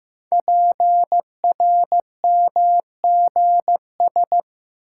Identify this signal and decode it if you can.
Morse code, heard as a keyed tone, about 15 wpm, 700 Hz. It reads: PRMGS
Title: Code Groups - 5 Letters - 15wpm